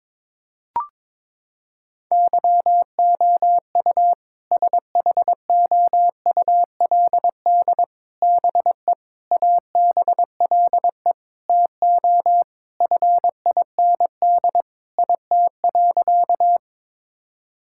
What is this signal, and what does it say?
Morse code, heard as a keyed tone, about 22 wpm, 700 Hz. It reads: YOU SHOULD BE ABLE TO FIND IT.
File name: practice / 22wpm / Sentences with 4-Word Phrases - 22wpm